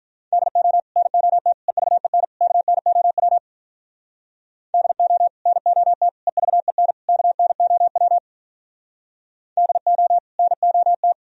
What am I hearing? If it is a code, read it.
Morse code, heard as a keyed tone, about 45 wpm, 700 Hz. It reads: DO NOT EVEN KNOW DO NOT EVEN KNOW DO NOT